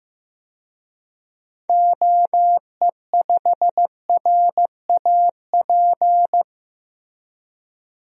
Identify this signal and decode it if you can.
Morse code, heard as a keyed tone, about 16 wpm, 700 Hz. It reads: OE5RAP
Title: DXCC-100 Call Signs - 6 Characters - 15wpm